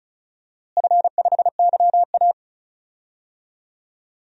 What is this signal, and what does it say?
Morse code, heard as a keyed tone, about 34 wpm, 700 Hz. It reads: F5YA
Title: DXCC-100 Call Signs - 4 Characters - Repeated 2x - 35wpm